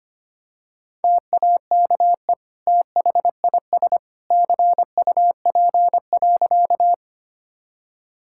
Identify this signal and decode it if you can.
Morse code, heard as a keyed tone, about 25 wpm, 700 Hz. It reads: TAKE THIS CUP.
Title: Sentences from Top 1600 Words 25wpm